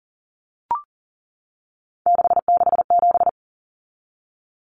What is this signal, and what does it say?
Morse code, heard as a keyed tone, about 40 wpm, 700 Hz. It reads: 667
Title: Common Cut Numbers - 3-Digit 40wpm